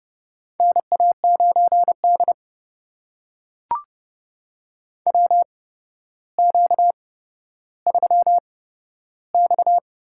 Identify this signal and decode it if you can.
Morse code, heard as a keyed tone, about 30 wpm, spaced as for 13 wpm, 700 Hz. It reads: NA9D WQ3X
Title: USA Call Signs - 8x Character Spacing - 30wpm